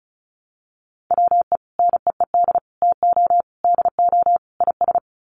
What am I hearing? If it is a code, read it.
Morse code, heard as a keyed tone, about 34 wpm, 700 Hz. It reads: WE NEED TO DO IS
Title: Sets of 5 Words Encore2 35wpm